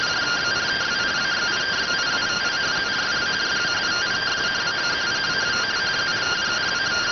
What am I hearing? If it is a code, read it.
Two-tone teleprinter signal, about 170 Hz shift.